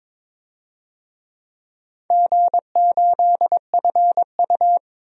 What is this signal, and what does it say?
Morse code, heard as a keyed tone, about 22 wpm, 700 Hz. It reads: G8FU
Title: DXCC-100 Call Signs - 4 Characters - 22wpm